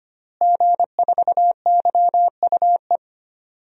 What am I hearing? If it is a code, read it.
Morse code, heard as a keyed tone, about 25 wpm, 700 Hz. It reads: G4YUE
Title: DXCC-100 Call Signs - 5 Characters - Repeated 2x - 25wpm